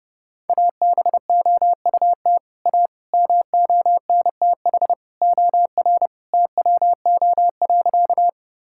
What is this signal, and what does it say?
Morse code, heard as a keyed tone, about 31 wpm, 700 Hz. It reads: ABOUT A MONTH OR TWO.
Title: Sentences from Top 1000 Words - Review Part 5 - 30wpm